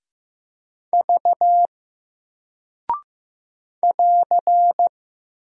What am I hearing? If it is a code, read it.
Morse code, heard as a keyed tone, about 15 wpm, 700 Hz. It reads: V <AR>
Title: Single Character (Letter:Number:Punctuation:Prosign) Alt 15wpm